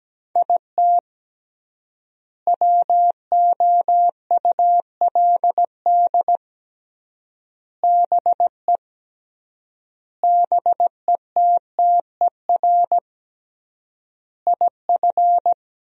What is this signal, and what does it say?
Morse code, heard as a keyed tone, about 17 wpm, 700 Hz. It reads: IT WOULD BE BETTER IF
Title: 3x Word-Spacing - Sentences from Top 400 Words - 17wpm